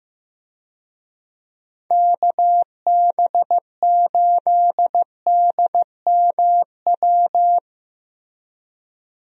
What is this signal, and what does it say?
Morse code, heard as a keyed tone, about 15 wpm, 700 Hz. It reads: KB8DMW
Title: USA Call Signs - 6 Characters - Repeated 2x - 15wpm